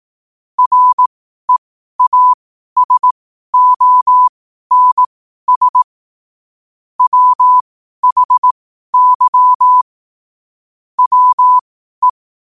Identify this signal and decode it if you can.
Morse code, heard as a keyed tone, about 18 wpm, spaced as for 12 wpm, 1000 Hz. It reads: REASONS WHY WE
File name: qotdfiles / 12wpm